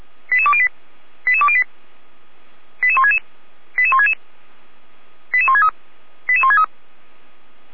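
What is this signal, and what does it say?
Five-tone selective call (ZVEI), heard as quick runs of five tones, tones of about 70 ms. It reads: 88208 88208 88166 88166 88163 88163
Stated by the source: ZVEI five-tone call received by SDR.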